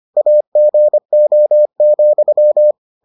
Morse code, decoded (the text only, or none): AGO,